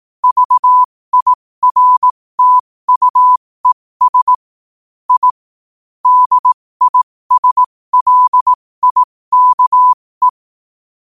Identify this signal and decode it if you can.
Morse code, heard as a keyed tone, about 18 wpm, spaced as for 15 wpm, 1000 Hz. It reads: VIRTUES I DISLIKE
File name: qotdfiles / 15wpm